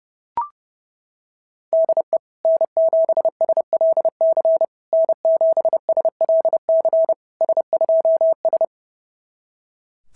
Morse code, heard as a keyed tone, about 26 wpm, 650 Hz. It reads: DE N7SLC N7SLC S2S
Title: SOTA QSO Practice 30wpm